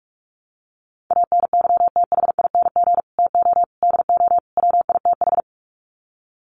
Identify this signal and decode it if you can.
Morse code, heard as a keyed tone, about 44 wpm, 700 Hz. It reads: ANYTHING TO DO WITH
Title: Sets of 4 Words 45wpm